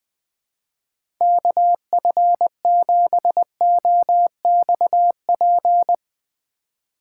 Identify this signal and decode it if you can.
Morse code, heard as a keyed tone, about 20 wpm, 700 Hz. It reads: KF7OXP
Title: USA Call Signs - 6 Characters - 3x Character Spacing - 20wpm